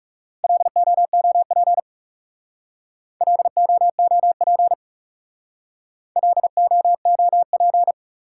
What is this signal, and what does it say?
Morse code, heard as a keyed tone, about 38 wpm, 700 Hz. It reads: LOOP LOOP LOOP